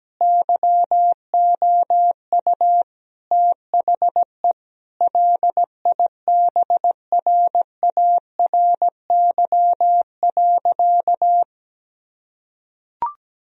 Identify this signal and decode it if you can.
Morse code, heard as a keyed tone, about 17 wpm, 700 Hz. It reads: YOU THE LIBRARY.
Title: Sentences from Top 1000 Words - Review Part 4 - 17wpm